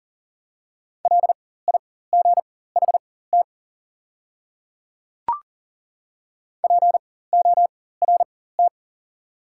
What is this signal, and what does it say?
Morse code, heard as a keyed tone, about 40 wpm, spaced as for 19 wpm, 690 Hz. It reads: LIGHT PORT
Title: 4x Character Spacing - Top 200 Words - 40wpm